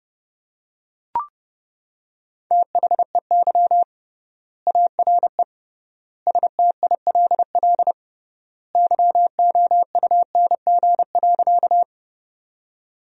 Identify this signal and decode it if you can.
Morse code, heard as a keyed tone, about 30 wpm, 700 Hz. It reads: THEY ARE STILL YOUNG.
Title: 3x Word-Spacing - Sentences from Top 400 Words - 30wpm